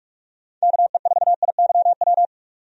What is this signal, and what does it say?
Morse code, heard as a keyed tone, about 46 wpm, 700 Hz. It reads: KE4IYW
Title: USA Call Signs - 6 Characters - 45wpm